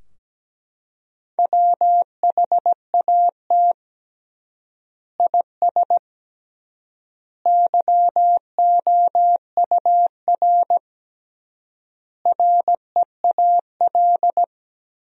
Morse code, heard as a keyed tone, about 17 wpm, 700 Hz. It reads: WHAT IS YOUR REAL
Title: 3x Word-Spacing - Sentences from Top 300 Words - 17wpm